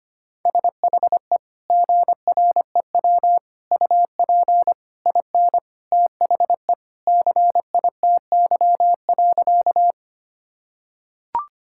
Morse code, heard as a keyed tone, about 25 wpm, 700 Hz. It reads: SHE GREW UP IN THE CITY.
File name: practice / 25wpm / Sentences with 2-Word Phrases - 25wpm